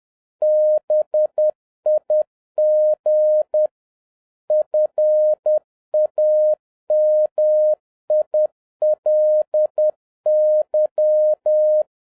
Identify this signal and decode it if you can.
Morse code, heard as a keyed tone, about 10 wpm, 610 Hz. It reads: BIG FAMILY